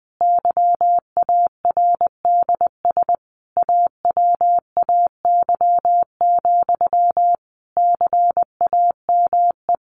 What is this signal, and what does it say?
Morse code, heard as a keyed tone, about 20 wpm, 700 Hz. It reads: YARDS AWAY, CAME
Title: A Sound Of Thunder 20wpm